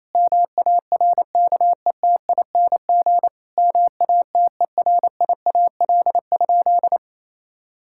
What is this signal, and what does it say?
Morse code, heard as a keyed tone, about 28 wpm, 700 Hz. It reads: MARKETING MATERIAL?
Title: Sentences from Top 1000 Words - Review Part 5 - 28wpm